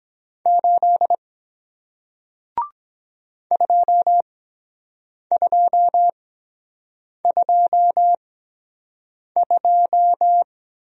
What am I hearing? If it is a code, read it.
Morse code, heard as a keyed tone, about 23 wpm, spaced as for 7 wpm, 700 Hz. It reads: 8 2222